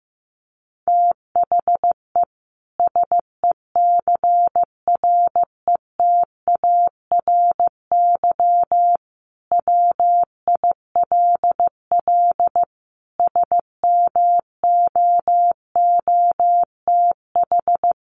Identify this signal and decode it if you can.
Morse code, heard as a keyed tone, about 15 wpm, 700 Hz. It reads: THE SECRETARY WILL SMOOTH
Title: Sentences from Top 1400 Words 15wpm